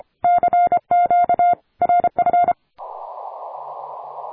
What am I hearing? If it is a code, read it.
Morse code, heard as a keyed tone, about 28 wpm, 700 Hz. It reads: CQ RF
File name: attachments / 40m UU7JF P